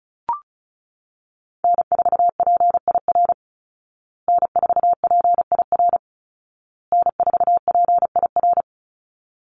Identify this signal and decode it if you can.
Morse code, heard as a keyed tone, about 34 wpm, 700 Hz. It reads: N4PIR N4PIR N4PIR
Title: Call Signs Repeated 2x - 35wpm